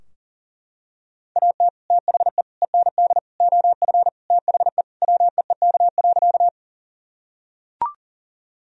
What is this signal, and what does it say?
Morse code, heard as a keyed tone, about 40 wpm, 700 Hz. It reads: AT THE END OF THE WEEK.
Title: Sentences from Top 1000 Words - Review Part 2 - 40wpm